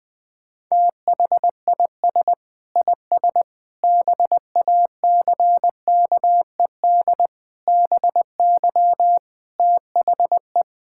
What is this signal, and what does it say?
Morse code, heard as a keyed tone, about 20 wpm, 700 Hz. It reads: THIS IS BACKED BY THE
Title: Sentences with 9-Word Phrases - 20wpm